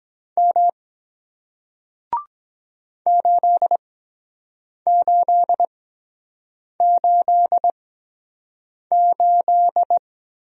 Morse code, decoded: M 8888

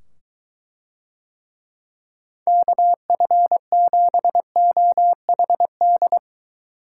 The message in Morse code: KF7OHD

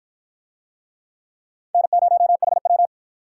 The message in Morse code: N0HW

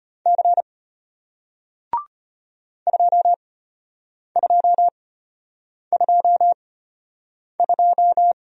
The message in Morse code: C 2222

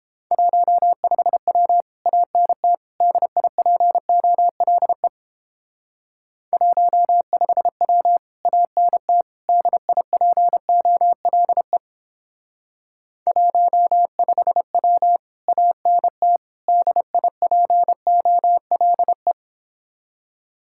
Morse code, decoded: 15W ANT DIPOLE 15W ANT DIPOLE 15W ANT DIPOLE

